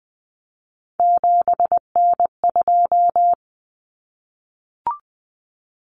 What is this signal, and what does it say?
Morse code, heard as a keyed tone, about 20 wpm, 700 Hz. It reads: 7N2
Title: Common Cut Numbers - 3-Digit 20wpm